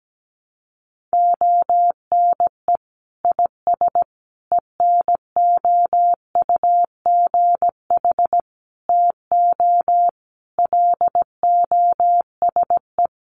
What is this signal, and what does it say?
Morse code, heard as a keyed tone, about 17 wpm, 700 Hz. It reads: ONE IS ENOUGH TO LOSE
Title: Sentences from Top 1100 Words 17wpm